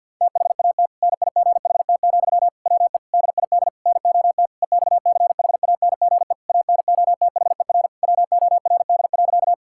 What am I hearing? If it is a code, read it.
Morse code, heard as a keyed tone, about 50 wpm, 700 Hz. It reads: THAT NIGHT, WE DID NOT EXCHANGE ANOTHER WORD.
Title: Sentences from Top 1000 Words - Review Part 6 - 50wpm